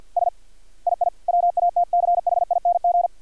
Morse code, heard as a keyed tone, about 50 wpm, 700 Hz. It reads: S II WATCHING
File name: archived / CRSnet